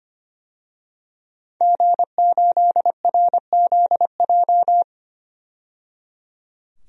Morse code, heard as a keyed tone, about 25 wpm, 700 Hz. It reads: G8RZJ
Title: DXCC-100 Call Signs - 5 Characters - 25wpm